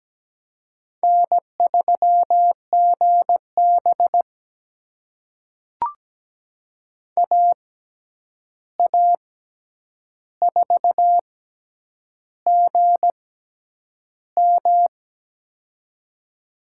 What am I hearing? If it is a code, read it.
Morse code, heard as a keyed tone, about 17 wpm, spaced as for 8 wpm, 700 Hz. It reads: N3GB AA4GM
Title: USA Call Signs - 6x Character Spacing - 17wpm